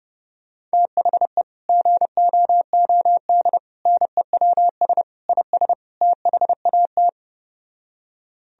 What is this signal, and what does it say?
Morse code, heard as a keyed tone, about 29 wpm, 700 Hz. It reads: THE GOOD NEWS IS THAT
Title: Speed-Racing - Sets of 5 Words 25wpm